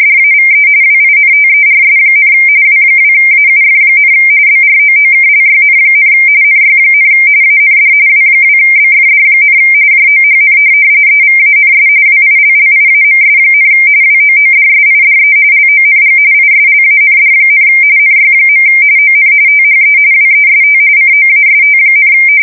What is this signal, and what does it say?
RTTY, about 45.45 baud. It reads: DAVE WINTERS, W2AUF, BROOKLYN, NY, AND W2BFD, JOHN EVANS WILLIAMS, WOODSIDE LONG ISLAND, NY. ON THE WEST COAST,
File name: fsk170 / teletype-RTTY